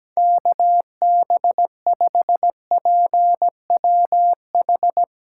Morse code, heard as a keyed tone, about 17 wpm, 700 Hz. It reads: KB5PWH
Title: USA Callsign DE Callsign 17wpm